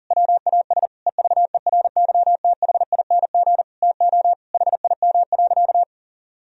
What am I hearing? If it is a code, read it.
Morse code, heard as a keyed tone, about 37 wpm, 700 Hz. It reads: WAS EVERYTHING TO HIM.
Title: Sentences from Top 1000 Words - Review Part 3 - 40wpm